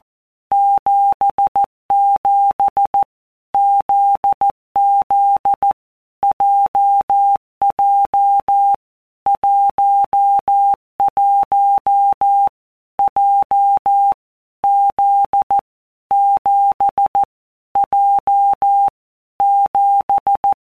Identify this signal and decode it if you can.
Morse code, heard as a keyed tone, about 14 wpm, 800 Hz. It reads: 77 ZZ JJ 11 J Z 7 J 7
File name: Basic Letter Groups By WPM's / Groups@13 / JZ17@13